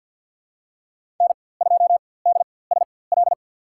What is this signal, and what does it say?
Morse code, heard as a keyed tone, about 46 wpm, spaced as for 23 wpm, 700 Hz. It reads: N2DSL
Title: USA Call Signs - 4x Character Spacing - 50wpm